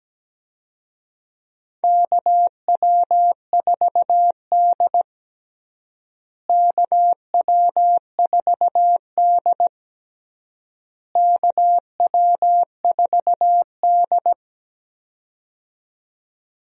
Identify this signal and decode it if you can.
Morse code, heard as a keyed tone, about 17 wpm, 700 Hz. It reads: KW4D KW4D KW4D
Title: USA Call Signs - 4 Characters - Repeated 3x - 17wpm